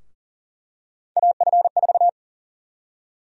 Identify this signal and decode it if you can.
Morse code, heard as a keyed tone, about 40 wpm, 700 Hz. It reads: AF4
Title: USA Call Sign Prefixes - Repeated 2x - 40wpm